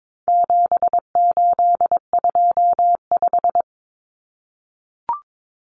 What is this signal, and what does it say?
Morse code, heard as a keyed tone, about 22 wpm, 700 Hz. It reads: 7825
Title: Numbers - 4-Digit 22wpm